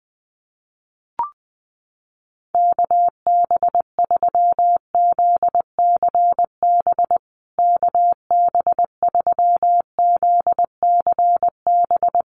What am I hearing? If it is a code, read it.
Morse code, heard as a keyed tone, about 20 wpm, 700 Hz. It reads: KB3ZCB KB3ZCB
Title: USA Call Signs Repeated 2x 20wpm